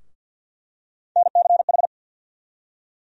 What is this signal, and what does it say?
Morse code, heard as a keyed tone, about 48 wpm, 700 Hz. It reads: NCH
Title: USA Call Sign Postfixes - 3 Letters 50wpm